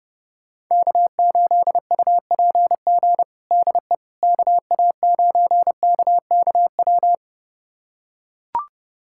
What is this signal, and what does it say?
Morse code, heard as a keyed tone, about 30 wpm, 700 Hz. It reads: K8UPG DE KA9KKW
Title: USA Callsign DE Callsign 30wpm